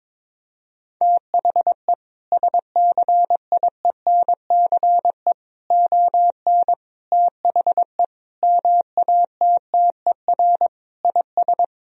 Morse code, heard as a keyed tone, about 22 wpm, 700 Hz. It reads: THE SCIENCE ON THE MATTER IS
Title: Sentences from Top 1000 Words - Review Part 2 - 22wpm